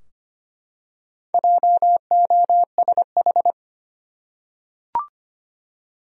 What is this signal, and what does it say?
Morse code, heard as a keyed tone, about 25 wpm, 700 Hz. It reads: JOSH
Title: USA Common Names - Max 5 Letters - 25wpm